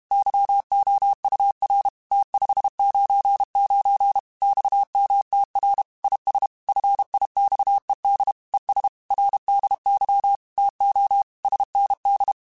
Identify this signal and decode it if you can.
Morse code, heard as a keyed tone, about 32 wpm, 800 Hz. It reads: YOUR T599 XMTR IS FIXED ES RDY TO SND
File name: QSOs / QSOs-mp3s / WA7QOX 1@30